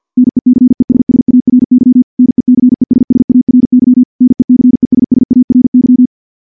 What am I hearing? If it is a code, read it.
Morse code, heard as a keyed tone, about 48 wpm, 250 Hz. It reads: NECESSARY NECESSARY NECESSARY
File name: morse / morse-training-50WPM